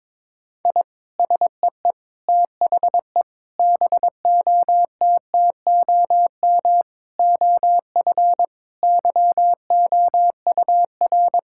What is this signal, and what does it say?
Morse code, heard as a keyed tone, about 22 wpm, 700 Hz. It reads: I SEE THE BOTTOM OF YOUR
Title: Sentences from Top 1000 Words - Review Part 6 - 22wpm